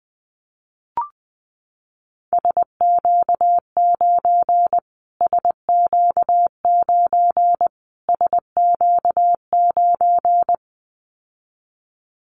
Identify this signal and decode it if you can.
Morse code, heard as a keyed tone, about 20 wpm, 700 Hz. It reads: SQ9 SQ9 SQ9
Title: Callsign Prefixes for Top 25 DXCC - Repeated 3x - 20wpm